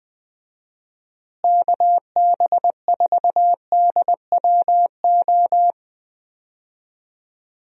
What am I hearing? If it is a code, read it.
Morse code, heard as a keyed tone, about 20 wpm, 700 Hz. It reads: KB4DWO
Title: USA Call Signs - 6 Characters - Repeated 2x - 20wpm